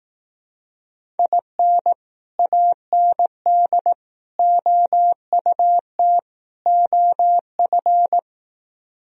Morse code, heard as a keyed tone, about 18 wpm, 700 Hz. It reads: IN AND OUT OF
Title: Speed-Racing - Sets of 4 Words 15wpm